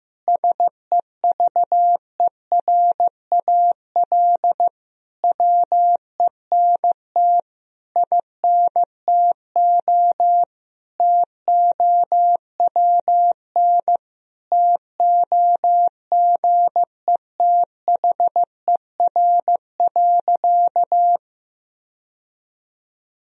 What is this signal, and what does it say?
Morse code, heard as a keyed tone, about 15 wpm, 700 Hz. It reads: SEVERAL WENT INTO TOWN TOGETHER.
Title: Sentences from Top 1000 Words - Review Part 2 - 15wpm